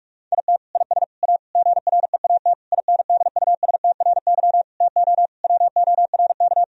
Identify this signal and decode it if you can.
Morse code, heard as a keyed tone, about 46 wpm, 700 Hz. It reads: IT IS A GREAT INDUSTRY TO WORK
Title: Sentences from Top 1000 Words - Review Part 4 - 45wpm